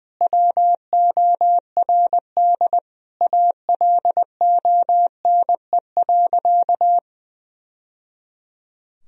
Morse code, heard as a keyed tone, about 21 wpm, 700 Hz. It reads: WORD ALONE.
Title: Sentences from Top 1000 Words - Review Part 2 - 20wpm